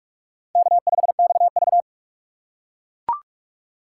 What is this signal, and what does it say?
Morse code, heard as a keyed tone, about 42 wpm, 700 Hz. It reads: K5XV
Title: USA Call Signs - 4 Characters - 6x Character Spacing - 45wpm